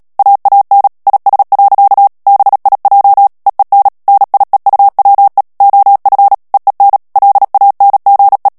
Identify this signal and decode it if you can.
Morse code, heard as a keyed tone, about 37 wpm, 800 Hz. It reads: AAN IS. BIJ EEN NIEUWE OF EEN LANGE
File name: morse mp3 / HOLLANDAIS texte clair 2 vitesse 35 mots minute